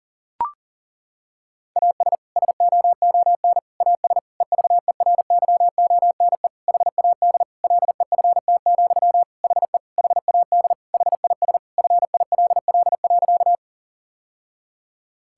Morse code, heard as a keyed tone, about 40 wpm, 700 Hz. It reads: AS SOON AS EVERYONE HAD LEFT, HE HAD HIS FILL.
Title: Sentences from Top 1000 Words - Review Part 5 - 40wpm